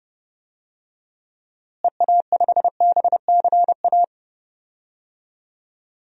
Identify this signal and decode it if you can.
Morse code, heard as a keyed tone, about 31 wpm, 700 Hz. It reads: EA5BCA
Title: DXCC-100 Call Signs - 6 Characters - Repeated 2x - 30wpm